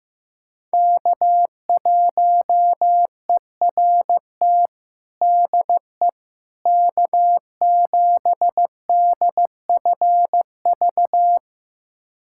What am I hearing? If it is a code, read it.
Morse code, heard as a keyed tone, about 15 wpm, 700 Hz. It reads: K1ERT DE K7DFV
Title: USA Callsign DE Callsign - 5 Characters - 15wpm